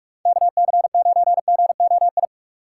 Morse code, heard as a keyed tone, about 59 wpm, spaced as for 52 wpm, 700 Hz. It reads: KC9GOI
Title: Speed Racing - USA 2x3 Call Signs 45wpm